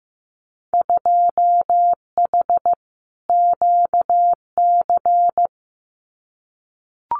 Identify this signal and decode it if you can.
Morse code, heard as a keyed tone, about 15 wpm, 700 Hz. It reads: 2H QC
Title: Winter Field Day Exchanges 15wpm